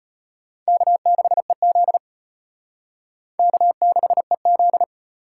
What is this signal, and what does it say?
Morse code, heard as a keyed tone, about 31 wpm, 700 Hz. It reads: K6EZ K6EZ
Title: Speed Racing - USA 1x2 Call Signs 25wpm